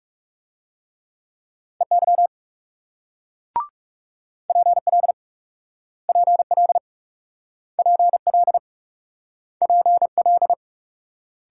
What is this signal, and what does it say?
Morse code, heard as a keyed tone, about 42 wpm, 700 Hz. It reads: EY PL PL PL PL